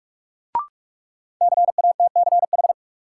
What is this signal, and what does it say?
Morse code, heard as a keyed tone, about 46 wpm, 700 Hz. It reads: CATCH
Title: Sentences from Top 1500 Words - New Word Drill Alt - 45wpm